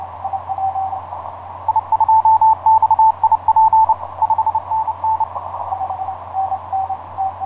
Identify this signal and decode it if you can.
Morse code, heard as a keyed tone, about 30 wpm, 880 Hz, with radio noise. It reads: I2XIP 5NN
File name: 8q7qx dk7lx etc 12m cw 30mar2010 1141